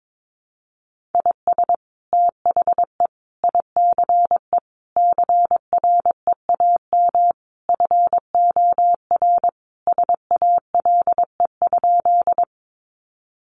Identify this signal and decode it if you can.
Morse code, heard as a keyed tone, about 22 wpm, 700 Hz. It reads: IS THE ICE CREAM FOR SALE?
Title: Sentences from Top 1500 Words 22wpm